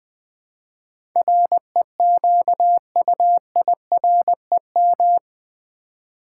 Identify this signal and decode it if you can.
Morse code, heard as a keyed tone, about 20 wpm, 700 Hz. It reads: REQUIREM
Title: Build A Word - 4 Letters At A Time - 11-Letter Words - 20wpm